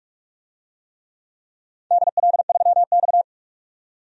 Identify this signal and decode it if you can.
Morse code, heard as a keyed tone, about 46 wpm, 700 Hz. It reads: DL3X
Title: DXCC-100 Call Signs - 4 Characters - 45wpm